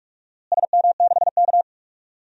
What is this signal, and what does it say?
Morse code, heard as a keyed tone, about 41 wpm, 700 Hz. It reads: SM6K
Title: DXCC-100 Call Signs - 4 Characters - Repeated 3x - 45wpm